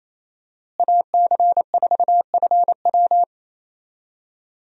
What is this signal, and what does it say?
Morse code, heard as a keyed tone, about 28 wpm, 700 Hz. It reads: AC4FW
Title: USA Call Signs - 5 Characters - Repeated 3x - 28wpm